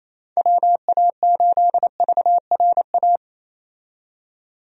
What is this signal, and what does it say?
Morse code, heard as a keyed tone, about 27 wpm, 700 Hz. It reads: WA8VRA
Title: USA Call Signs - 6 Characters - 28wpm